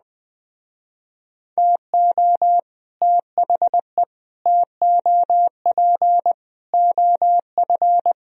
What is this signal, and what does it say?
Morse code, heard as a keyed tone, about 20 wpm, 700 Hz. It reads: TO THE TOP OF
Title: Speed-Racing - Sets of 4 Words 17wpm